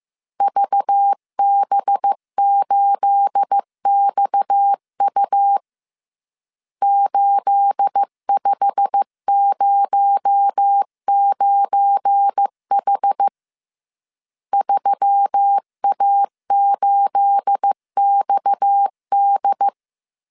Morse code, heard as a keyed tone, about 15 wpm, 800 Hz. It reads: VB8XU 8509H 3A8XD
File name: CW / DET / 1 9 1500C S C 14wpm 3245